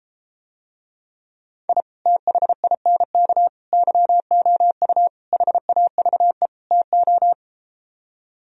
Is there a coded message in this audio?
Yes: Morse code from 1.7 to 7.3 s, heard as a keyed tone, about 33 wpm, 700 Hz. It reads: I THINK YOU HAVE TO